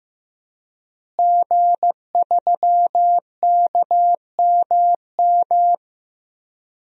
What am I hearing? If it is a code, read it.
Morse code, heard as a keyed tone, about 15 wpm, 700 Hz. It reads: G3KMM
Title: DXCC-100 Call Signs - 5 Characters - Repeated 3x - 15wpm